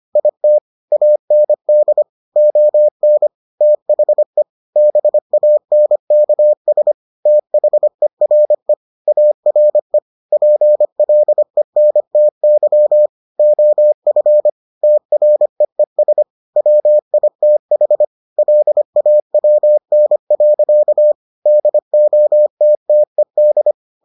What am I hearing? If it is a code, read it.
Morse code, heard as a keyed tone, about 25 wpm, 600 Hz. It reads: IT AND ON THE BANKS THERE ARE PLENTY OF TREES WITH LAWN. DOTTED